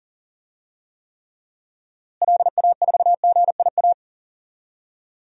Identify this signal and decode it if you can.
Morse code, heard as a keyed tone, about 40 wpm, 700 Hz. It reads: LA4GIA
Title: DXCC-100 Call Signs - 6 Characters - Repeated 2x - 40wpm